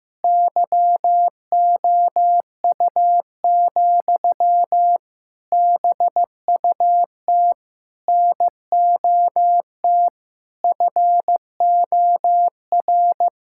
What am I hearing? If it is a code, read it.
Morse code, heard as a keyed tone, about 15 wpm, 700 Hz. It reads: YOU, BUT NOT FOR
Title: Sentences from Top 1000 Words - Review Part 5 - 15wpm